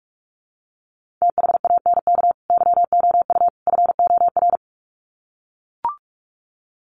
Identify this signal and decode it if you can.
Morse code, heard as a keyed tone, about 44 wpm, 700 Hz. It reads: THANK YOU FOR
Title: Sets of 3 Words 45wpm